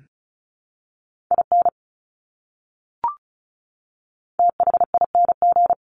Morse code, heard as a keyed tone, about 34 wpm, 700 Hz. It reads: IN THING
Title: Top 200 Words Repeated 35wpm